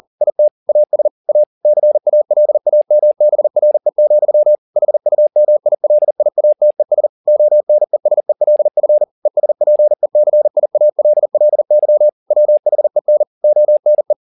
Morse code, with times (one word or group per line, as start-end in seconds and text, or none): 0.2-0.5 IT
0.7-1.1 AS
1.3-1.4 A
1.6-4.5 CALAMBRE,
4.8-7.1 HUMILIATES
7.3-9.0 ONESELF
9.3-12.1 ESPECIALLY
12.3-13.2 WHEN
13.4-14.1 ONE